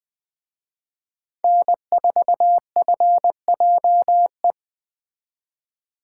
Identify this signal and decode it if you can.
Morse code, heard as a keyed tone, about 24 wpm, 700 Hz. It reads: N4FJE
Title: Speed Racing - USA 1x3 Call Signs 20wpm